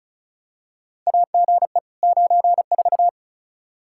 Morse code, heard as a keyed tone, about 36 wpm, 700 Hz. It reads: AGE 94
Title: Practice QSO - Exchange 3 - More Info - 35wpm